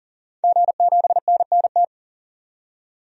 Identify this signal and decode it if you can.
Morse code, heard as a keyed tone, about 40 wpm, 700 Hz. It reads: G7NNT
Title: DXCC-100 Call Signs - 5 Characters - 40wpm